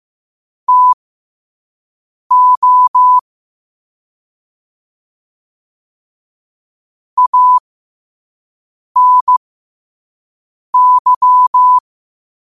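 Morse code, heard as a keyed tone, about 15 wpm, spaced as for 5 wpm, 1000 Hz. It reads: TO ANY
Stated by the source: Quote-of-the-day Morse code practice at 5 wpm.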